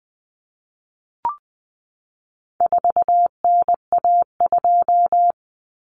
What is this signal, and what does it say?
Morse code, heard as a keyed tone, about 20 wpm, 700 Hz. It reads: 4NA2
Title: Common Cut Numbers - 4-Digit 20wpm